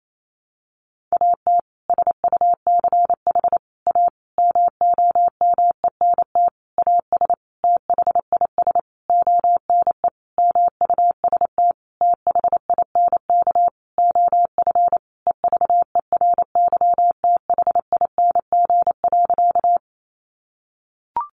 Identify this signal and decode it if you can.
Morse code, heard as a keyed tone, about 28 wpm, 700 Hz. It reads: AT SUCH A MOMENT AS THIS ONE MUST THINK OF EVERYTHING.
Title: Sentences from Top 1000 Words 28wpm